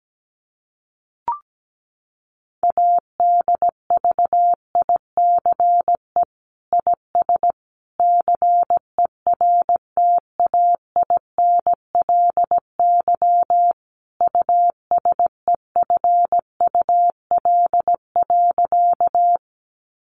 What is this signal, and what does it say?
Morse code, heard as a keyed tone, about 17 wpm, 700 Hz. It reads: ADVICE IS CERTAINLY USEFUL.
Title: Sentences from Top 1100 Words 17wpm